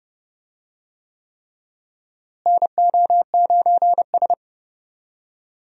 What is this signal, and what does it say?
Morse code, heard as a keyed tone, about 30 wpm, 700 Hz. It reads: NO9S
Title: USA Call Signs - 4 Characters - Repeated 3x - 30wpm